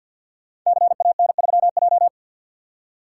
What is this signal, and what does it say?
Morse code, heard as a keyed tone, about 48 wpm, 700 Hz. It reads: CAN3J
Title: Code Groups - 5 Letter:Numbers - 50wpm